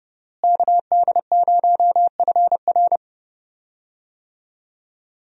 Morse code, heard as a keyed tone, about 30 wpm, 700 Hz. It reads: KD0FR
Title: USA Call Signs - 5 Characters - Repeated 3x - 30wpm